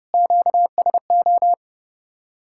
Morse code, heard as a keyed tone, about 37 wpm, 700 Hz. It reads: QSO
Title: Speed-Racing Common QSO Phrases Encore - 30wpm